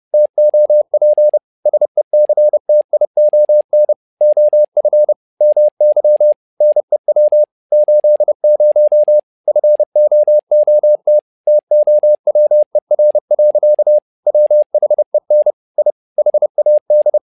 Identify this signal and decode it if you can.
Morse code, heard as a keyed tone, about 30 wpm, 600 Hz. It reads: TOP SECTION OF MY NEW 80 FOOT TOWER. WHEN I HAD
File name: learning center / tower trouble / tower trouble 30wpm